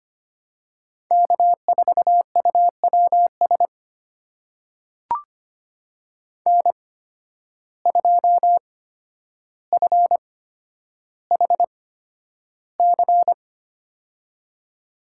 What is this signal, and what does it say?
Morse code, heard as a keyed tone, about 25 wpm, spaced as for 11 wpm, 700 Hz. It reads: K4UWS N2FHC
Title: USA Call Signs - 5 Characters - 8x Character Spacing - 25wpm